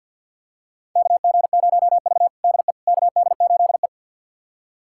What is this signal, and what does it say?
Morse code, heard as a keyed tone, about 46 wpm, 700 Hz. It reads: KG0V DE KD8E
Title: USA Callsign DE Callsign - 4 Characters - 50wpm